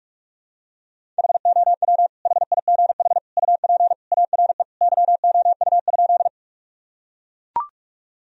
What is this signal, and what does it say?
Morse code, heard as a keyed tone, about 41 wpm, 700 Hz. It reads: HOW HIGH UP ARE YOU?
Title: Sentences from Top 1000 Words - Review Part 1 - 45wpm